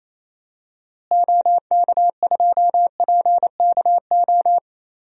Morse code, heard as a keyed tone, about 27 wpm, 700 Hz. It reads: OK2PKO
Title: DXCC-100 Call Signs - 6 Characters - Repeated 2x - 28wpm